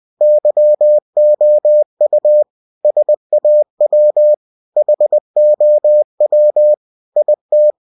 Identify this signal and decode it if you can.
Morse code, heard as a keyed tone, about 20 wpm, 600 Hz. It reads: YOU SAW HOW IT